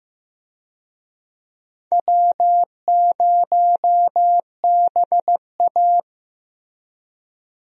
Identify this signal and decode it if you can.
Morse code, heard as a keyed tone, about 15 wpm, 700 Hz. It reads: W0BA
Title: USA Call Signs - 4 Characters - Repeated 2x - 15wpm